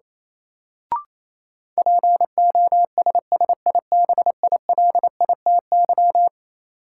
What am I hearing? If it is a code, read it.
Morse code, heard as a keyed tone, about 28 wpm, 700 Hz. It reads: POSSIBILITY
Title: Sets of 3 Words - New Word Drill - 28wpm